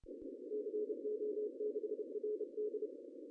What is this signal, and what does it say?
Morse code, heard as a keyed tone, about 30 wpm, 420 Hz. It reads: CO6RD
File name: CW / cqww-cw-k5tr